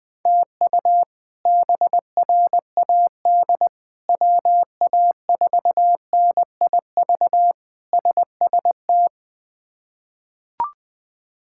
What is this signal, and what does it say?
Morse code, heard as a keyed tone, about 20 wpm, 700 Hz. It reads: TU BRAD WA4NIV SST
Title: K1USN SST QSO Practice - 20wpm